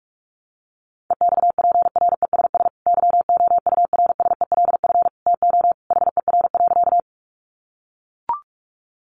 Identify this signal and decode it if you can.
Morse code, heard as a keyed tone, about 44 wpm, 700 Hz. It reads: EXPRESS YOURSELF TO HER.
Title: Sentences from Top 1100 Words 45wpm